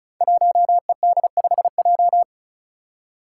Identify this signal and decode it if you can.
Morse code, heard as a keyed tone, about 34 wpm, 700 Hz. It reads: 1ED5J
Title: Five Challenging Letter-Number Combinations 35wpm